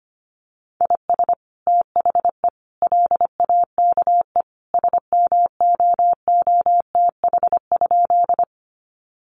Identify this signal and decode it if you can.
Morse code, heard as a keyed tone, about 25 wpm, 700 Hz. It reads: IS THE LAKE SMOOTH?
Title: Sentences from Top 1400 Words 25wpm